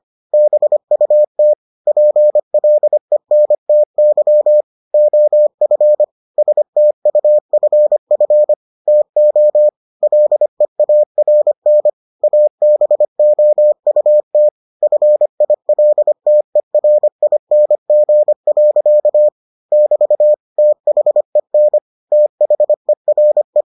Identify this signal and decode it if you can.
Morse code, heard as a keyed tone, about 25 wpm, 600 Hz. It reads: BUT PLENTY OF STUFF TO LEARN ABOUT FILTERING. <BT> THEN THERE